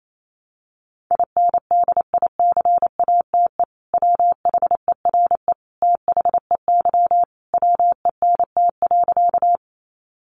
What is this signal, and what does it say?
Morse code, heard as a keyed tone, about 28 wpm, 700 Hz. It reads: INDICATE WHERE THEY WENT.
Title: Sentences from Top 1300 Words 28wpm